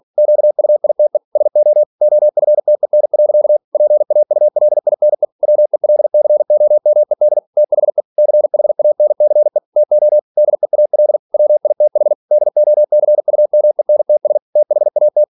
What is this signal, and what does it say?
Morse code, heard as a keyed tone, about 46 wpm, 600 Hz. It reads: QUITE SO OFTEN. PAULINE WELCOMED THE CHANCE TO DEAL WITH DOCUMENTS THAT
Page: Never 47wpm